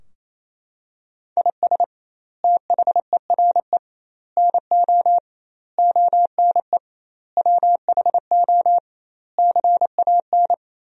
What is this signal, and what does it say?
Morse code, heard as a keyed tone, about 28 wpm, 700 Hz. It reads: IS THERE NO ONE WHO CAN
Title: 2x Word-Spacing - Sentences from Top 200 Words - 28wpm